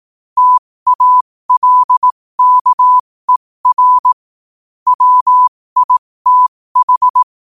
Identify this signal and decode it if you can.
Morse code, heard as a keyed tone, about 18 wpm, spaced as for 15 wpm, 1000 Hz. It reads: TALKER WITH
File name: qotdfiles / 15wpm